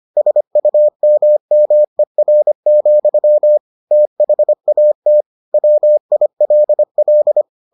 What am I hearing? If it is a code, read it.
Morse code, heard as a keyed tone, about 25 wpm, 600 Hz. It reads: SUMMER, THAT WILL